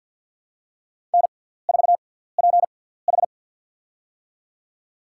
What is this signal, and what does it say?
Morse code, heard as a keyed tone, about 50 wpm, spaced as for 17 wpm, 700 Hz. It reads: N4PH
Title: USA Call Signs - 6x Character Spacing - 50wpm